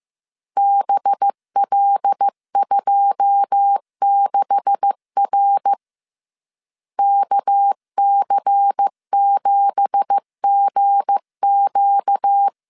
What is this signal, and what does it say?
Morse code, heard as a keyed tone, about 15 wpm, 800 Hz. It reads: BL26R KC7GQ
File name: CW / DET / 1 9 1500C S C 14wpm 3245